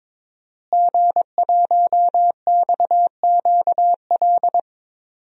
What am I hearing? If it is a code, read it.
Morse code, heard as a keyed tone, about 22 wpm, 700 Hz. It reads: G1XQL
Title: DXCC-100 Call Signs - 5 Characters - Repeated 3x - 22wpm